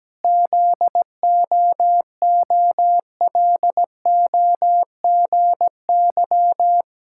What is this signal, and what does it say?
Morse code, heard as a keyed tone, about 17 wpm, 700 Hz. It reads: ZOOLOGY
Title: Pangram Challenge v2 - 17wpm